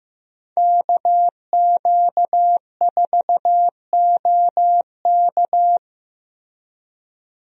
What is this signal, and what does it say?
Morse code, heard as a keyed tone, about 15 wpm, 700 Hz. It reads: KQ4OK
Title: USA Call Signs - 5 Characters - Repeated 3x - 15wpm